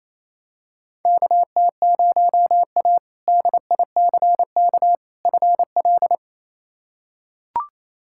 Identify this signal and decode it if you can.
Morse code, heard as a keyed tone, about 28 wpm, 700 Hz. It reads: KT0A DICK FL
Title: K1USN SST QSO Practice - 28wpm